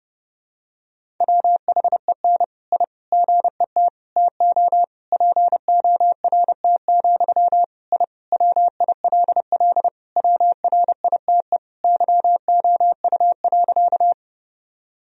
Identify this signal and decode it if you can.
Morse code, heard as a keyed tone, about 30 wpm, 700 Hz. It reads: WHEN I GET TO PORT, I WILL WRITE YOU.
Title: Sentences from Top 1000 Words - Review Part 1 - 30wpm